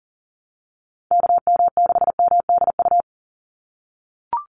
Morse code, heard as a keyed tone, about 40 wpm, 700 Hz. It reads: KM6MDU
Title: USA Call Signs 40wpm